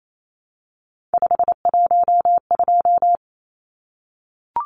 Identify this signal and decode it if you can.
Morse code, heard as a keyed tone, about 28 wpm, 700 Hz. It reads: 512
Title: Numbers - 3-Digit 28wpm